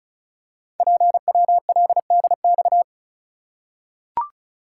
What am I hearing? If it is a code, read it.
Morse code, heard as a keyed tone, about 36 wpm, 700 Hz. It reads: PWLDX
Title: Code Groups - 5 Letter:Numbers - 35wpm